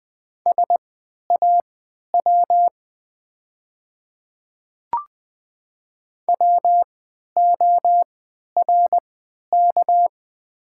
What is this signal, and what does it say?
Morse code, heard as a keyed tone, about 20 wpm, spaced as for 11 wpm, 700 Hz. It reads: SAW WORK
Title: 3x Character Spacing - Top 300 Words - 20wpm